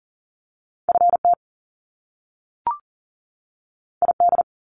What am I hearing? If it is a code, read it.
Morse code, heard as a keyed tone, about 42 wpm, 700 Hz. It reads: FT ID
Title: Top Two Letter Combinations in Words 40wpm